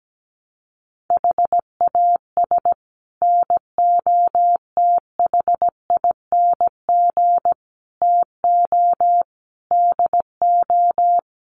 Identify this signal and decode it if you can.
Morse code, heard as a keyed tone, about 17 wpm, 700 Hz. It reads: HAS NOTHING TO DO
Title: Sets of 5 Words 17wpm